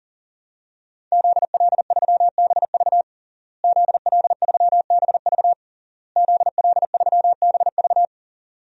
Morse code, heard as a keyed tone, about 40 wpm, 700 Hz. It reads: ZL3BV ZL3BV ZL3BV
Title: DXCC-100 Call Signs - 5 Characters - Repeated 3x - 40wpm